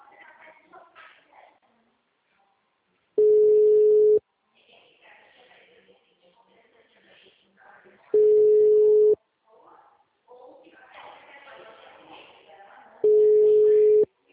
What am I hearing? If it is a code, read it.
Ringback tone (Europe, 425 Hz), with speech.